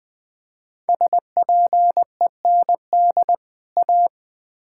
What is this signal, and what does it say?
Morse code, heard as a keyed tone, about 20 wpm, 700 Hz. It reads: SPEND A
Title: Build A Phrase - 5-Word Phrases - 20wpm